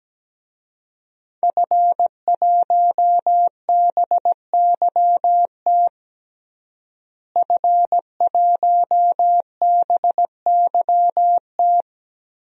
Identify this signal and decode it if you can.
Morse code, heard as a keyed tone, about 17 wpm, 700 Hz. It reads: F1BYT F1BYT
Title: DXCC-100 Call Signs - 5 Characters - Repeated 3x - 17wpm